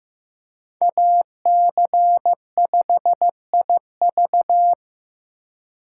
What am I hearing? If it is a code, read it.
Morse code, heard as a keyed tone, about 15 wpm, 700 Hz. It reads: AC5IV
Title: USA Call Signs - 5 Characters - 2x Character Spacing - 15wpm